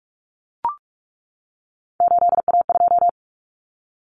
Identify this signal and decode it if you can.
Morse code, heard as a keyed tone, about 44 wpm, 700 Hz. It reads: 8A2
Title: Common Cut Numbers - 3-Digit 45wpm